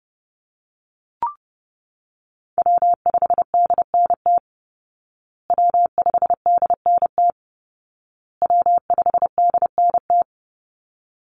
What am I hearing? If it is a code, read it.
Morse code, heard as a keyed tone, about 30 wpm, 700 Hz. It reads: W5DNT W5DNT W5DNT
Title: Call Signs Repeated 2x - 30wpm